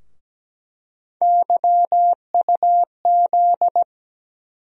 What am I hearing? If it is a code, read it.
Morse code, heard as a keyed tone, about 17 wpm, 700 Hz. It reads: YUZ
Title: USA Call Sign Postfixes - 3 Letters 17wpm